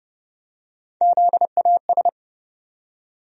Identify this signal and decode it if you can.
Morse code, heard as a keyed tone, about 30 wpm, 700 Hz. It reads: ZAS
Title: USA Call Sign Postfixes - 3 Letters 30wpm